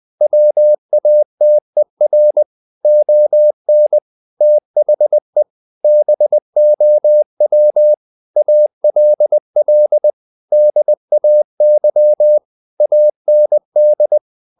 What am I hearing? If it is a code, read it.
Morse code, heard as a keyed tone, about 20 wpm, 600 Hz. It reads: WATER ON THE BOW ALL DAY AND
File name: the old man and the sea part 2 20wpm